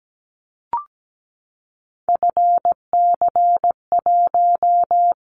Morse code, heard as a keyed tone, about 17 wpm, 700 Hz. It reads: FC1
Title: Callsign Prefixes for Top 50 DXCC - 17wpm